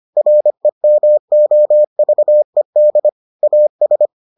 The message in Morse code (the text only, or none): REMOVED AS